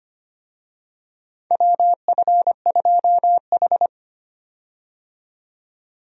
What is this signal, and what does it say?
Morse code, heard as a keyed tone, about 25 wpm, 700 Hz. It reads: WF2H
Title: USA Call Signs - 4 Characters - 25wpm